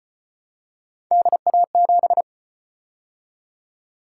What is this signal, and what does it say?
Morse code, heard as a keyed tone, about 34 wpm, 700 Hz. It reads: DA7